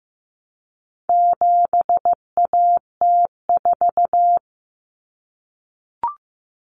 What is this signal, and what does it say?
Morse code, heard as a keyed tone, about 15 wpm, 700 Hz. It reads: 7AT4
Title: Common Cut Numbers - 4-Digit 15wpm